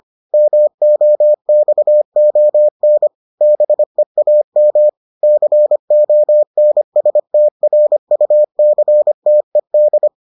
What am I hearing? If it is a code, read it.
Morse code, heard as a keyed tone, about 25 wpm, 600 Hz. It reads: MOXON BEAM CONSTRUCTED